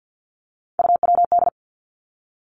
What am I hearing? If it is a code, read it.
Morse code, heard as a keyed tone, about 50 wpm, 700 Hz. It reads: UWD
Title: Three Challenging Letter-Number Combinations 50wpm